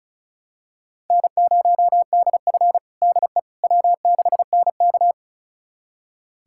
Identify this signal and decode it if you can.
Morse code, heard as a keyed tone, about 36 wpm, 700 Hz. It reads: N0DF DE W6NK
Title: USA Callsign DE Callsign - 4 Characters - 35wpm